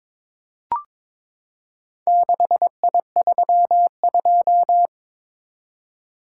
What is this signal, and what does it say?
Morse code, heard as a keyed tone, about 22 wpm, 700 Hz. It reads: 6I32
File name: practice / 22wpm / Four Letter-Number Combinations - 22wpm